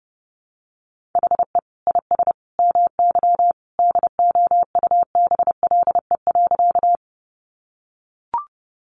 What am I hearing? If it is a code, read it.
Morse code, heard as a keyed tone, about 30 wpm, 700 Hz. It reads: HE IS MY DOUBLE.
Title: Sentences from Top 1400 Words 30wpm